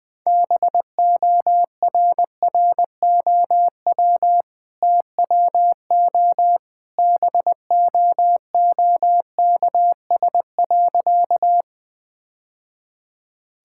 Morse code, heard as a keyed tone, about 19 wpm, 700 Hz. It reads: BORROW TWO BOOKS.
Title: Easy Sentences with Hints - 20wpm